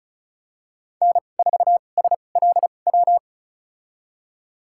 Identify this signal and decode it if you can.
Morse code, heard as a keyed tone, about 36 wpm, spaced as for 25 wpm, 700 Hz. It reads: N4SLW
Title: USA Call Signs - 5 Characters - 2x Character Spacing - 35wpm